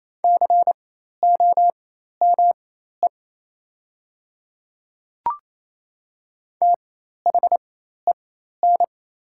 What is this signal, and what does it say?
Morse code, heard as a keyed tone, about 28 wpm, spaced as for 13 wpm, 700 Hz. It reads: COME THEN
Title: 4x Character Spacing - Top 200 Words - 28wpm